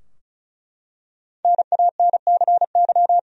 Morse code, heard as a keyed tone, about 36 wpm, 700 Hz. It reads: NANCY
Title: USA Common Names - 35wpm